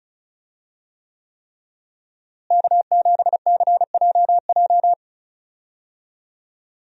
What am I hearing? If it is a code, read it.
Morse code, heard as a keyed tone, about 36 wpm, 700 Hz. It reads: K7CJJ
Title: USA Call Signs - 5 Characters - Repeated 3x - 35wpm